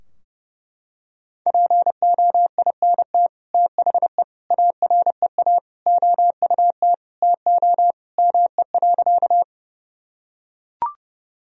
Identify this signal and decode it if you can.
Morse code, heard as a keyed tone, about 30 wpm, 700 Hz. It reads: POINT THE AREA OUT TO ME.
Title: Sentences from Top 1000 Words - Review Part 2 - 30wpm